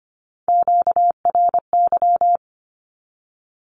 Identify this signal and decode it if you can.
Morse code, heard as a keyed tone, about 25 wpm, 700 Hz. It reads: QRY
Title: Q-Codes 25wpm